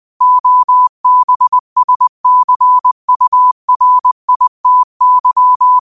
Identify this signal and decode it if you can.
Morse code, heard as a keyed tone, about 20 wpm, 1000 Hz. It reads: OBSCURITY